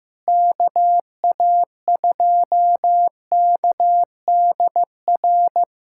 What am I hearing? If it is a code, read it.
Morse code, heard as a keyed tone, about 15 wpm, 700 Hz. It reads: KA2KDR
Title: USA Callsign DE Callsign 15wpm